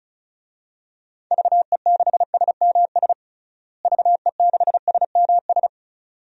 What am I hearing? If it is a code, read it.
Morse code, heard as a keyed tone, about 36 wpm, 700 Hz. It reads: VE6SMS VE6SMS
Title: DXCC-100 Call Signs - 6 Characters - Repeated 2x - 35wpm